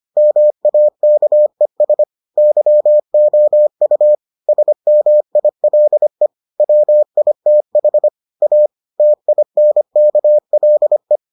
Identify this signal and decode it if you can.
Morse code, heard as a keyed tone, about 25 wpm, 600 Hz. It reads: MAKES YOU SMILE WITH A TINKLE